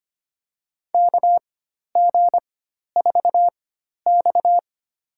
Morse code, heard as a keyed tone, about 25 wpm, spaced as for 12 wpm, 700 Hz. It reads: KG4X
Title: USA Call Signs - 4 Characters - 4x Character Spacing - 25wpm